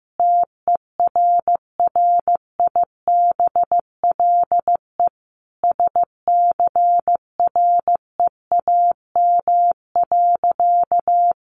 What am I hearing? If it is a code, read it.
Morse code, heard as a keyed tone, about 15 wpm, 700 Hz. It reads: TERRIBLE SCREAM.
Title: A Sound Of Thunder 15wpm